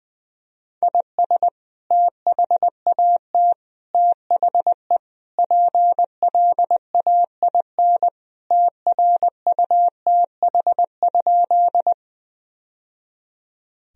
Morse code, heard as a keyed tone, about 21 wpm, 700 Hz. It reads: IS THAT THE PLAIN TRUTH?
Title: Sentences from Top 1000 Words - Review Part 2 - 20wpm